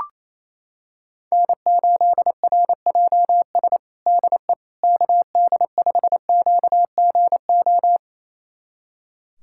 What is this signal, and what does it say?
Morse code, heard as a keyed tone, about 27 wpm, 700 Hz. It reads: N8RJS DE KD5QGO
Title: USA Callsign DE Callsign 28wpm